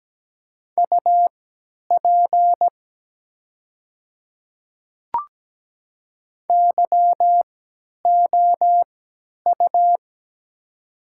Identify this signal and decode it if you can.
Morse code, heard as a keyed tone, about 17 wpm, spaced as for 10 wpm, 700 Hz. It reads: UP YOU
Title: 3x Character Spacing - Top 100 Words - 17wpm